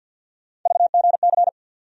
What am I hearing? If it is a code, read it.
Morse code, heard as a keyed tone, about 53 wpm, 700 Hz. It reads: VGC
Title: USA Call Sign Postfixes - 3 Letters 50wpm